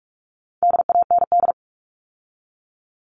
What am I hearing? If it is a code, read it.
Morse code, heard as a keyed tone, about 44 wpm, 700 Hz. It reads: DAND
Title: All Cut Numbers - 4-Digit 45wpm